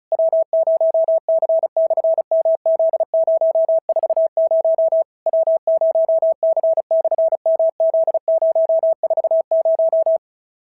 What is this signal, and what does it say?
Morse code, heard as a keyed tone, about 33 wpm, 650 Hz. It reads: W0C/MZ040 W0C/MZ040
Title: SOTA QSO Practice 35wpm